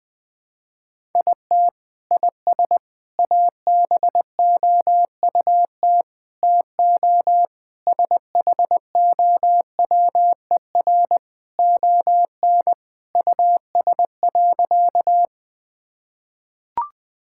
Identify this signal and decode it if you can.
Morse code, heard as a keyed tone, about 20 wpm, 700 Hz. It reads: IT IS ABOUT TO SHOWER ON US.
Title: Sentences from Top 1600 Words 20wpm